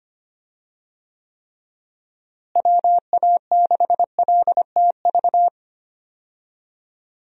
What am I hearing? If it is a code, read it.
Morse code, heard as a keyed tone, about 25 wpm, 700 Hz. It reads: WA6LTV
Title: USA Call Signs - 6 Characters - Repeated 3x - 25wpm